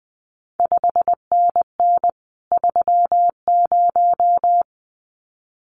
Morse code, heard as a keyed tone, about 20 wpm, 700 Hz. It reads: 5NN 30
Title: CQ WW DX Contest Exchange Encore - 20wpm